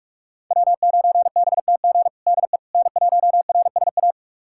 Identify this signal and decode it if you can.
Morse code, heard as a keyed tone, about 42 wpm, 700 Hz. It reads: W9BTG DE N1RSA
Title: USA Callsign DE Callsign - 5 Characters - 45wpm